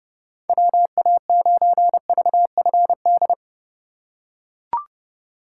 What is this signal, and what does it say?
Morse code, heard as a keyed tone, about 30 wpm, 700 Hz. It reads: WA9VFD